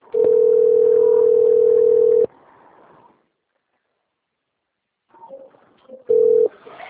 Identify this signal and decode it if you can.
Ringback tone (North America), 440 + 480 Hz.
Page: Itu - SP